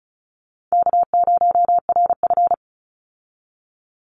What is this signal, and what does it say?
Morse code, heard as a keyed tone, about 34 wpm, 700 Hz. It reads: K0RF
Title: Easy Call Signs Repeated 3x - 35wpm